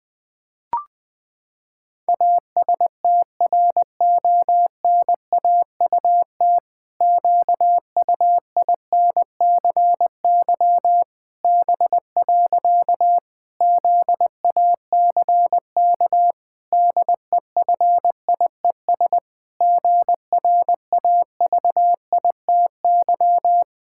Morse code, heard as a keyed tone, about 20 wpm, 700 Hz. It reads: ASTRONAUT QUINCY B. ZACK DEFIES GRAVITY
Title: Pangram Challenge v2 - 20wpm